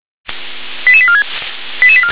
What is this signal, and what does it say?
Five-tone selective call (ZVEI), heard as quick runs of five tones, tones of about 70 ms. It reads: 88846